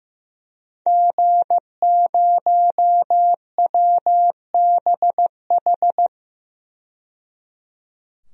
Morse code, heard as a keyed tone, about 16 wpm, 700 Hz. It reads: G0WBH
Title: DXCC-100 Call Signs - 5 Characters - 15wpm